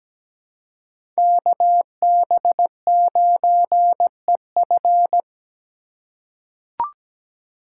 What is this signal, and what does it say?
Morse code, heard as a keyed tone, about 17 wpm, 700 Hz. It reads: KB9EF
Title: USA Call Signs - 5 Characters - 4x Character Spacing - 17wpm